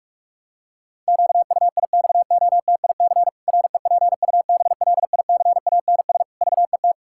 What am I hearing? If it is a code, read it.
Morse code, heard as a keyed tone, about 46 wpm, 700 Hz. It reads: QUIXOTIC REPUBLICANS VET
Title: Pangram Challenge v2 - 45wpm